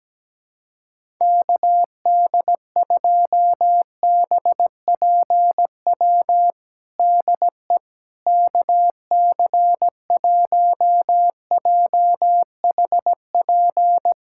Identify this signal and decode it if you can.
Morse code, heard as a keyed tone, about 17 wpm, 700 Hz. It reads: KD2BPW DE KC1JHP
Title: USA Callsign DE Callsign 17wpm